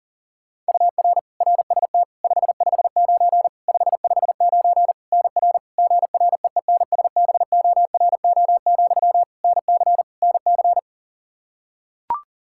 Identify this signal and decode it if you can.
Morse code, heard as a keyed tone, about 40 wpm, 700 Hz. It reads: UR RST 559 559 NR GREENSBORO, NC NC
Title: K1USN SST QSO Practice 40wpm